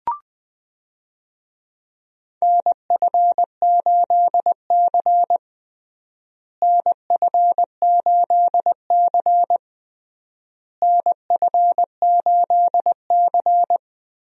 Morse code, heard as a keyed tone, about 20 wpm, 700 Hz. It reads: NF8C NF8C NF8C
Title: USA Call Signs - 4 Characters - Repeated 3x - 20wpm